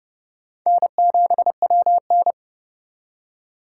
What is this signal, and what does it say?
Morse code, heard as a keyed tone, about 38 wpm, 700 Hz. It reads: N7WN